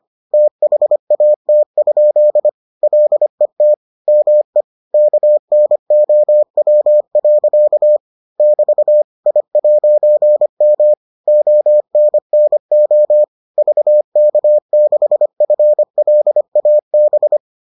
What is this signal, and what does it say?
Morse code, heard as a keyed tone, about 25 wpm, 600 Hz. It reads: THAT? LET ME KNOW. <BT> I'M ONNO VK6FLAB